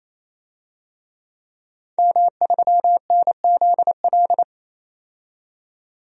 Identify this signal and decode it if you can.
Morse code, heard as a keyed tone, about 28 wpm, 700 Hz. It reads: M3NZL
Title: DXCC-100 Call Signs - 5 Characters - 28wpm